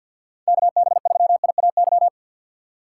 Morse code, heard as a keyed tone, about 52 wpm, 690 Hz. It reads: KB3IAY
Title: USA Call Signs - 3x Character Spacing - 50wpm